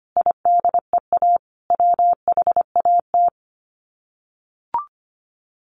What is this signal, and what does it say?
Morse code, heard as a keyed tone, about 25 wpm, 700 Hz. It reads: IDEA WHAT
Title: Sets of 5 Words Encore 25wpm